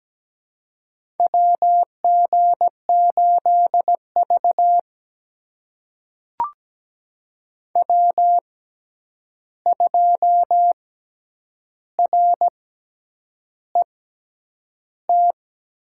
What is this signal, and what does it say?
Morse code, heard as a keyed tone, about 17 wpm, spaced as for 8 wpm, 700 Hz. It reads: WG8V W2RET
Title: USA Call Signs - 6x Character Spacing - 17wpm